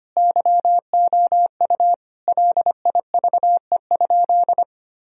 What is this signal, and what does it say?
Morse code, heard as a keyed tone, about 25 wpm, 700 Hz. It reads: YOU LIVE?
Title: Sentences from Top 1000 Words - Review Part 4 - 25wpm